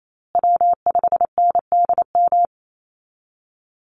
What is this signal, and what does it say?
Morse code, heard as a keyed tone, about 28 wpm, 700 Hz. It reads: W5NDM
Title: USA Call Signs Repeated 2x 28wpm